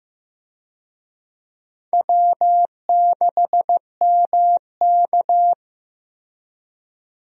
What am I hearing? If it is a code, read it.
Morse code, heard as a keyed tone, about 15 wpm, 700 Hz. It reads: W6MK
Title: USA Call Signs - 4 Characters - 15wpm